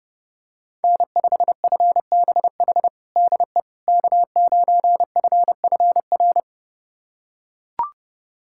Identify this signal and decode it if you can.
Morse code, heard as a keyed tone, about 30 wpm, 700 Hz. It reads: N5FBH DE K9FFR
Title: USA Callsign DE Callsign - 5 Characters - 30wpm